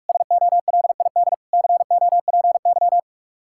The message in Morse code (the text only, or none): SOLID COPY